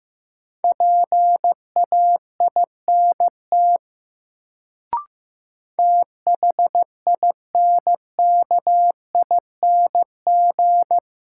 Morse code, heard as a keyed tone, about 15 wpm, 700 Hz. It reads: PAINT THINKING
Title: Sentences from Top 1000 Words - Review Part 6 - New Word Drill - 15wpm